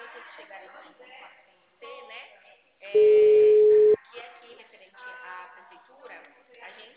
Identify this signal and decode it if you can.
Ringback tone (Europe, 425 Hz), with speech.
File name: out- -2044- - - .6702